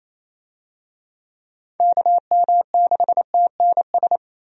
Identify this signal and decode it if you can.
Morse code, heard as a keyed tone, about 28 wpm, 700 Hz. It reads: KM6TNS
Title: USA Call Signs - 6 Characters - Repeated 2x - 28wpm